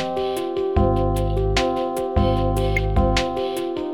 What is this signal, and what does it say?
There is no code.